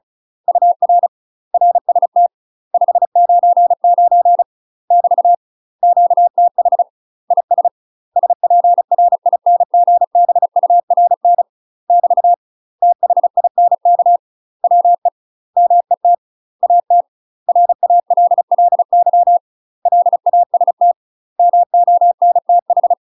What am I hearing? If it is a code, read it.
Morse code, heard as a keyed tone, about 34 wpm, 700 Hz. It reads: UR RST 599 <BT> QTH IS SPRINGBURN <BT> THINK WE MET AT RALLY LAST MONTH
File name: UK MASTER QSOs TRAINING / 35WPM / UK MASTER QSOs 0039